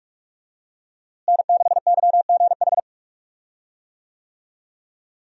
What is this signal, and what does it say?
Morse code, heard as a keyed tone, about 46 wpm, 700 Hz. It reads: N6YGH
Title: USA Call Signs - 5 Characters - Repeated 3x - 45wpm